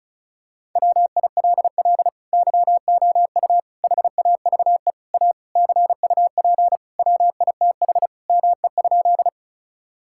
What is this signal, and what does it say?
Morse code, heard as a keyed tone, about 36 wpm, 700 Hz. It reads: WILL YOU HAVE A CUP WITH ME?
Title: Sentences from Top 1600 Words 35wpm